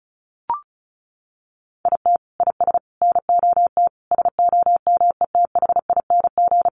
Morse code, heard as a keyed tone, about 34 wpm, 700 Hz. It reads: IT IS NOT SOMETHING
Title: Sentences from Top 1400 Words 35wpm